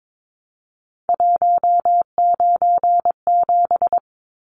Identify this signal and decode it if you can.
Morse code, heard as a keyed tone, about 22 wpm, 700 Hz. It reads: 197
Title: Numbers - 3-Digit 22wpm